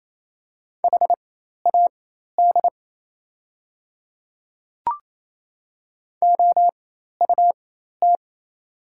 Morse code, heard as a keyed tone, about 28 wpm, spaced as for 13 wpm, 700 Hz. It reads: HAD OUT
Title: 4x Character Spacing - Top 100 Words - 28wpm